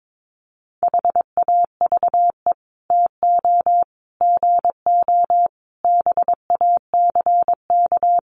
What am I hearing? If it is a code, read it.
Morse code, heard as a keyed tone, about 22 wpm, 700 Hz. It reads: HAVE TO GO BACK
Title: Sets of 5 Words Encore2 22wpm